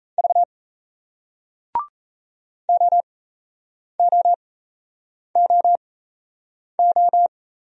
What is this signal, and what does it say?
Morse code, heard as a keyed tone, about 38 wpm, spaced as for 8 wpm, 700 Hz. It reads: V OOOO